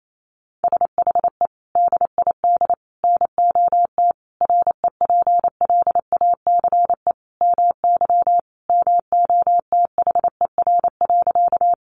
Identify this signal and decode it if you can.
Morse code, heard as a keyed tone, about 28 wpm, 700 Hz. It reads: SHE DID NOT REPLACE MY MOTHER.
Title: Sentences from Top 1400 Words 28wpm